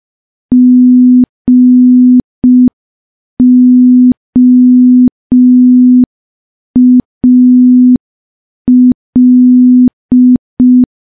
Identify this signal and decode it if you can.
Morse code, heard as a keyed tone, about 5 wpm, 250 Hz. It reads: GOAL
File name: morse / morse-training-05WPM